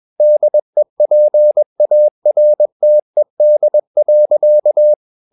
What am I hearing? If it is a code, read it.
Morse code, heard as a keyed tone, about 21 wpm, 600 Hz. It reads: DEPARTED.
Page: CW med Gnister